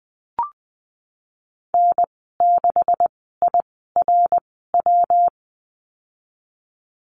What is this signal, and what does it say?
Morse code, heard as a keyed tone, about 20 wpm, spaced as for 14 wpm, 700 Hz. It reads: N6IRW
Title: 154 Lesson 25g - Letter p - Call Signs 20wpm